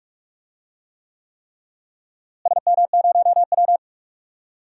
Morse code, heard as a keyed tone, about 41 wpm, 700 Hz. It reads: SM0W
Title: DXCC-100 Call Signs - 4 Characters - Repeated 3x - 45wpm